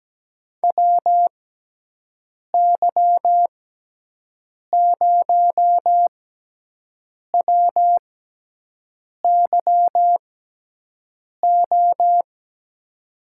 Morse code, heard as a keyed tone, about 17 wpm, 700 Hz. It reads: W Y 0 W Y O
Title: USA Call Signs - 6 Characters - 6x Character Spacing - 17wpm